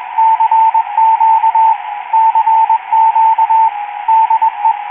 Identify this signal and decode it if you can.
Morse code, heard as a keyed tone, about 21 wpm, 890 Hz, with radio noise. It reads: CQ CQ DE